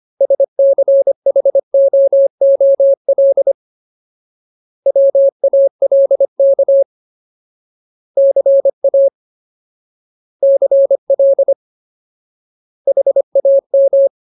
Morse code, heard as a keyed tone, about 25 wpm, 550 Hz. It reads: SCHOOL WALK CA CL HAM
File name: Ultimate Word List / 25WPM Ultimate